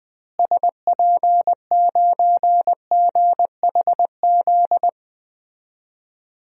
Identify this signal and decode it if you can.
Morse code, heard as a keyed tone, about 20 wpm, 700 Hz. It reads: SP9GHZ
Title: DXCC-100 Call Signs - 6 Characters - Repeated 2x - 20wpm